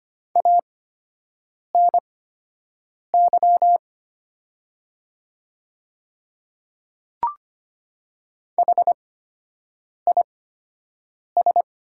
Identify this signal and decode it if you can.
Morse code, heard as a keyed tone, about 25 wpm, spaced as for 7 wpm, 700 Hz. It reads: ANY HIS
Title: 8x Character Spacing - Top 200 Words - 25wpm